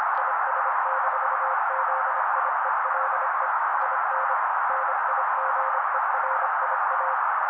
Morse code, heard as a keyed tone, about 25 wpm, 530 Hz. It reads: IS-ZIELE IN NIGERIA
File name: daily / 25wpm noise